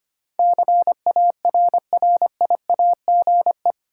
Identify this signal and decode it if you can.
Morse code, heard as a keyed tone, about 25 wpm, 700 Hz. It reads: CARRIAGE
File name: Eight Letter Words - 25wpm